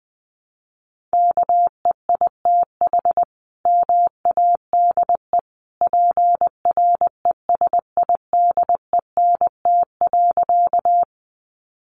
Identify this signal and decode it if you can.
Morse code, heard as a keyed tone, about 20 wpm, 700 Hz. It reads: KEITH MADE PRESIDENT.
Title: A Sound Of Thunder 20wpm